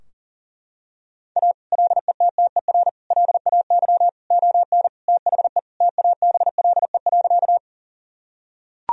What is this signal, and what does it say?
Morse code, heard as a keyed tone, about 40 wpm, 700 Hz. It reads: A LETTER LAY ON THE TABLE.
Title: Sentences from Top 1000 Words - Review Part 2 - 40wpm